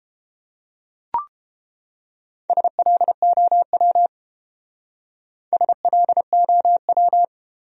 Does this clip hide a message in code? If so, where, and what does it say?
Yes: Morse code from 2.5 to 7.2 s, heard as a keyed tone, about 29 wpm, 700 Hz. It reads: SLOW SLOW